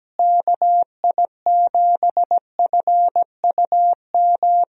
Morse code, heard as a keyed tone, about 17 wpm, 700 Hz. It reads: KI7FUM
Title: USA Call Signs - 6 Characters - Repeated 2x - 17wpm